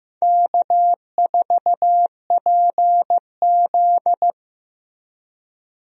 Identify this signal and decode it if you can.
Morse code, heard as a keyed tone, about 15 wpm, 700 Hz. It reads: K4PZ
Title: USA Call Signs - 4 Characters - Repeated 3x - 15wpm